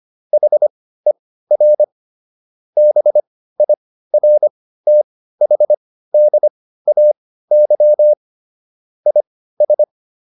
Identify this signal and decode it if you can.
Morse code, heard as a keyed tone, about 25 wpm, spaced as for 15 wpm, 600 Hz. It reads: HER BIRTHDAY IS